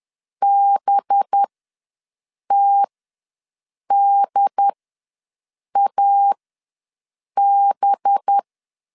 Morse code, heard as a keyed tone, about 10 wpm, spaced as for 6 wpm, 800 Hz. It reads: BTDAB
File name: LEZIONI / 7 GF9Z / 7 GF9Z ws 10wpm 3919